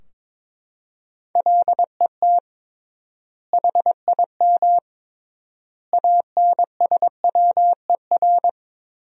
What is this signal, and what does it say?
Morse code, heard as a keyed tone, about 22 wpm, 700 Hz. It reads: LET HIM ANSWER
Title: 3x Word-Spacing - Sentences from Top 300 Words - 22wpm